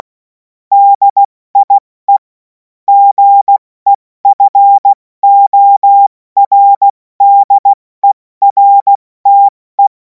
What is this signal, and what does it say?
Morse code, heard as a keyed tone, about 16 wpm, 800 Hz. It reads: DIE GEFORDERTE
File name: Morsekurs / Witze / Witz 10. Februar 2026 70 F80